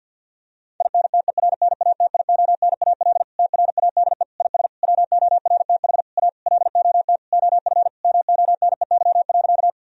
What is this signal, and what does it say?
Morse code, heard as a keyed tone, about 50 wpm, 700 Hz. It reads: INTERNATIONAL TRADE IS WORTH A LOT OF MONEY.
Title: Sentences from Top 1000 Words - Review Part 4 - 50wpm